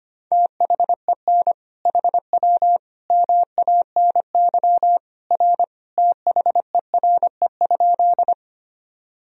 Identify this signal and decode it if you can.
Morse code, heard as a keyed tone, about 25 wpm, 700 Hz. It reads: THEN HW MANY R THERE?
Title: Sentences from Top 100 Words and Abbreviations 25wpm